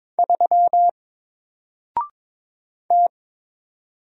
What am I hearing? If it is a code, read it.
Morse code, heard as a keyed tone, about 22 wpm, 700 Hz. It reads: 3 T
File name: Single-Letter-Number-22wpm